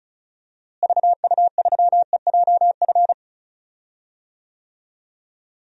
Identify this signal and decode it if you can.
Morse code, heard as a keyed tone, about 36 wpm, 700 Hz. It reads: VU3EJF
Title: DXCC-100 Call Signs - 6 Characters - Repeated 2x - 35wpm